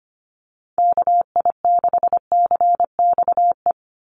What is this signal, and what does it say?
Morse code, heard as a keyed tone, about 25 wpm, 700 Hz. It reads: KI6CXE
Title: USA Call Signs 25wpm